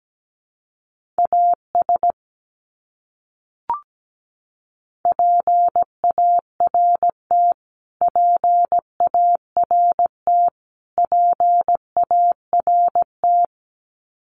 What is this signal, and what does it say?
Morse code, heard as a keyed tone, about 17 wpm, 700 Hz. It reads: AS PART PART PART
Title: Top 200 Words Repeated 17wpm